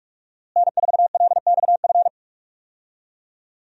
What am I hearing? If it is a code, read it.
Morse code, heard as a keyed tone, about 42 wpm, 700 Hz. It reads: N4LXF
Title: USA Call Signs - 5 Characters - Repeated 2x - 45wpm